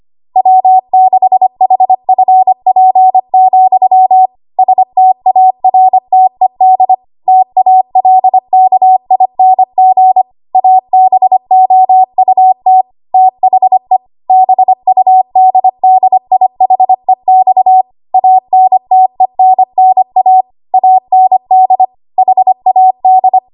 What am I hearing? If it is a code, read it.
Morse code, heard as a keyed tone, about 25 wpm, 750 Hz. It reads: W6HFP, STARTED TALKING ABOUT THE BUDDIHEX ANTENNA AND HAD
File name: Archive / 25 WPM / 25WPM